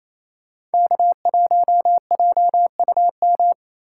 Morse code, heard as a keyed tone, about 36 wpm, 700 Hz. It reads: K1JUM